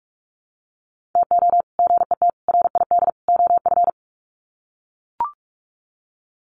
Morse code, heard as a keyed tone, about 44 wpm, 700 Hz. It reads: TO GET RID OF
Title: Sets of 4 Words 45wpm